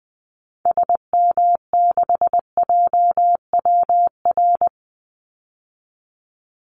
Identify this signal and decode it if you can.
Morse code, heard as a keyed tone, about 20 wpm, 700 Hz. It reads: SM6JWR
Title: Call Signs Repeated 3x - 20wpm